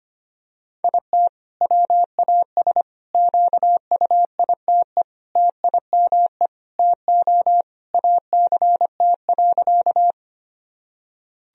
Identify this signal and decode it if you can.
Morse code, heard as a keyed tone, about 25 wpm, 700 Hz. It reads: IT WAS QUITE TIME TO ACT.
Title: Sentences from Top 1000 Words - Review Part 1 - 25wpm